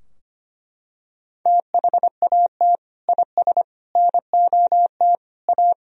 Morse code, heard as a keyed tone, about 25 wpm, 700 Hz. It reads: THAT IS NOT A
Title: Sentences from Top 1000 Words - Review Part 3 - 25wpm